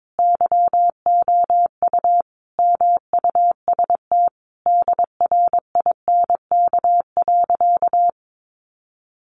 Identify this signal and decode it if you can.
Morse code, heard as a keyed tone, about 22 wpm, 700 Hz. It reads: YOU MUST DRINK.
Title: Sentences from Top 1500 Words 22wpm